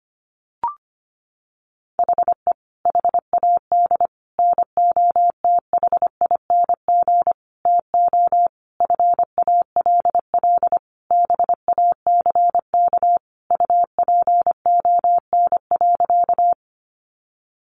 Morse code, heard as a keyed tone, about 25 wpm, 700 Hz. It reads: HE HAD NOTHING TO FALL BACK UPON.
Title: Sentences from Top 1100 Words 25wpm